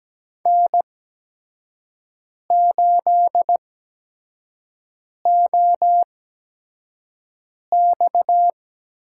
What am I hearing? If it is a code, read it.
Morse code, heard as a keyed tone, about 17 wpm, spaced as for 5 wpm, 700 Hz. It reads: N8OX
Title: USA Call Signs - 8x Character Spacing - 17wpm